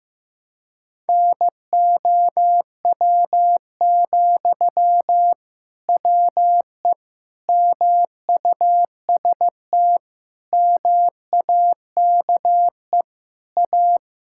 Morse code, heard as a keyed tone, about 15 wpm, 700 Hz. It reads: NOW, WE MUST MAKE A
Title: Sentences from Top 1000 Words - Review Part 6 - 15wpm